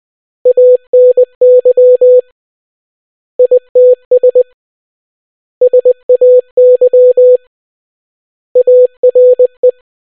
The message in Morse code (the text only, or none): ANY ITS SAY ARE